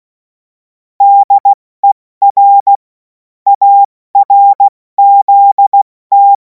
Morse code, heard as a keyed tone, about 16 wpm, 800 Hz. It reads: DER ARZT
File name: Morsekurs / Witze / Witz 07. Juli 2026 70 F80